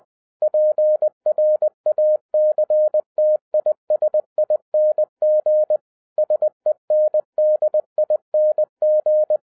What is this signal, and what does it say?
Morse code, heard as a keyed tone, about 20 wpm, 610 Hz. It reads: PRACTISING SENDING
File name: 2011-10-mp3 / n13-TIPS FROM MY 34 YEARS OF EXPERIENCE 20wpm